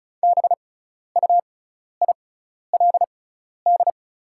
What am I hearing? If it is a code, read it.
Morse code, heard as a keyed tone, about 34 wpm, spaced as for 12 wpm, 700 Hz. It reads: BUILD
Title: 6x Character Spacing - Top 300 Words - 35wpm